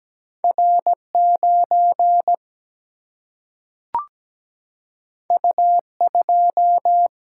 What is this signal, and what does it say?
Morse code, heard as a keyed tone, about 17 wpm, 700 Hz. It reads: R 9 U 2
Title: Single-Letter Single-Number 17wpm